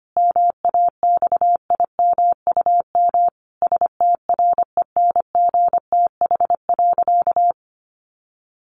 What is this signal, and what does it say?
Morse code, heard as a keyed tone, about 25 wpm, 700 Hz. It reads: MAXIMUM STRENGTH.
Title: Sentences from Top 1300 Words 25wpm